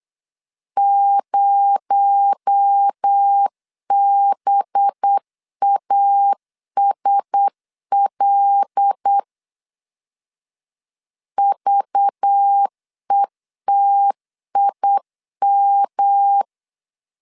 Morse code, heard as a keyed tone, about 8 wpm, 800 Hz. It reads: 0BASL VETIM
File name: CW / DET / 1 6 1500C S B 8wpm 5142